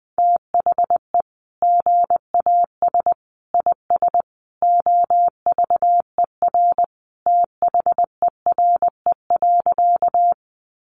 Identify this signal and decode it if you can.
Morse code, heard as a keyed tone, about 20 wpm, 700 Hz. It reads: THE GAS IS OVER THERE.
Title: Sentences from Top 1100 Words 20wpm